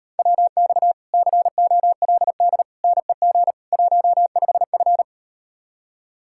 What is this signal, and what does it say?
Morse code, heard as a keyed tone, about 38 wpm, 700 Hz. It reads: WX COLD NEG 15F